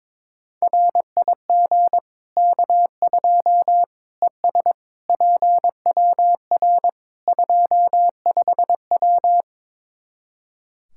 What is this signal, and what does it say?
Morse code, heard as a keyed tone, about 22 wpm, 700 Hz. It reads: RIG K2 ES PWR 25W
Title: Practice QSO - Exchange 2 - Info - 22wpm